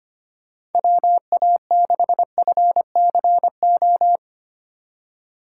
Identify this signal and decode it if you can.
Morse code, heard as a keyed tone, about 33 wpm, 700 Hz. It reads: WA6FCO